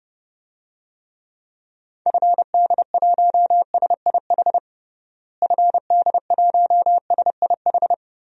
Morse code, heard as a keyed tone, about 30 wpm, 700 Hz. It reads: FD1SIH FD1SIH
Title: DXCC-100 Call Signs - 6 Characters - Repeated 3x - 30wpm